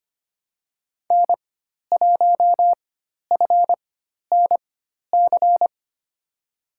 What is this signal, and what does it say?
Morse code, heard as a keyed tone, about 25 wpm, spaced as for 12 wpm, 700 Hz. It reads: N1FNC
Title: USA Call Signs - 5 Characters - 4x Character Spacing - 25wpm